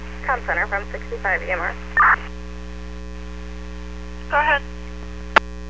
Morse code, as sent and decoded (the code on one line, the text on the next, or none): none